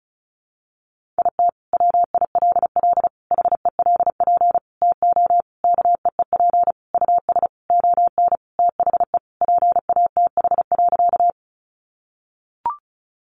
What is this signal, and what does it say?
Morse code, heard as a keyed tone, about 34 wpm, 700 Hz. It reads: IT WILL HELP TO KEEP US ON THE PATH.
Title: Sentences from Top 1400 Words 35wpm